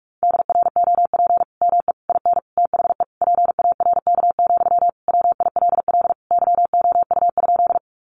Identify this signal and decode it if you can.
Morse code, heard as a keyed tone, about 44 wpm, 700 Hz. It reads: DROP ME IN THE PARK, WILL YOU?
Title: Sentences from Top 1400 Words 45wpm